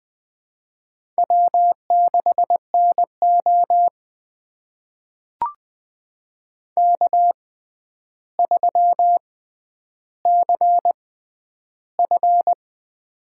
Morse code, heard as a keyed tone, about 20 wpm, spaced as for 10 wpm, 700 Hz. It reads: W6NO K3CF
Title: USA Call Signs - 6x Character Spacing - 20wpm